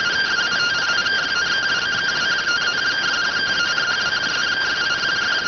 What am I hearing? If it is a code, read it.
Two-tone teleprinter signal, about 170 Hz shift.